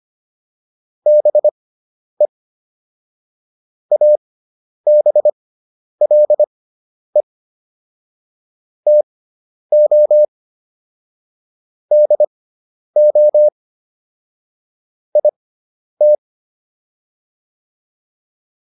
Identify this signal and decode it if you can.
Morse code, heard as a keyed tone, about 25 wpm, spaced as for 10 wpm, 600 Hz. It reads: BE ABLE TO DO IT